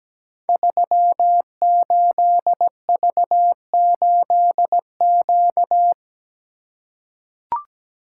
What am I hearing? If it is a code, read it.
Morse code, heard as a keyed tone, about 17 wpm, 700 Hz. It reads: 38V8Q
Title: Code Groups - 5 Letter:Numbers - 17wpm